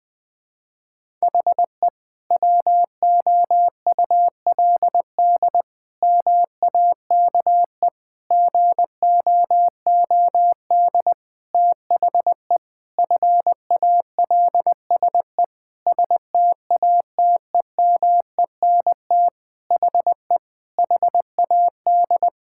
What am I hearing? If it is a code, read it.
Morse code, heard as a keyed tone, about 21 wpm, 700 Hz. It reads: HE WOULD MAKE GOOD THE FALSE STATEMENT HE HAD
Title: Sentences from Top 1600 Words 20wpm